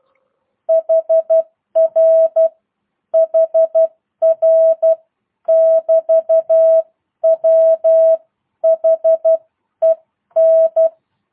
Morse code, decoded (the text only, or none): HR HR <BT> WHEN